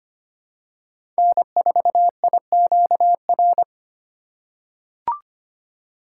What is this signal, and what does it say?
Morse code, heard as a keyed tone, about 25 wpm, 700 Hz. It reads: N4IQR
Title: USA Call Signs - 5 Characters - 2x Character Spacing - 25wpm